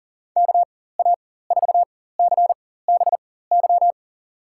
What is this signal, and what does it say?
Morse code, heard as a keyed tone, about 41 wpm, spaced as for 19 wpm, 700 Hz. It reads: KA4CBY
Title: USA Call Signs - 6 Characters - 4x Character Spacing - 40wpm